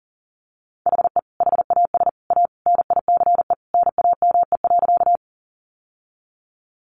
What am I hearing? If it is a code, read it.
Morse code, heard as a keyed tone, about 40 wpm, 700 Hz. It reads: HE HAS A NICE NAME.
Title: Sentences from Top 1100 Words 40wpm